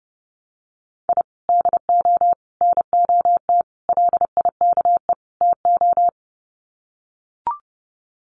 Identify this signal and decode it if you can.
Morse code, heard as a keyed tone, about 30 wpm, 700 Hz. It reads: I DO NOT LIKE TO
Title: Sets of 5 Words 30wpm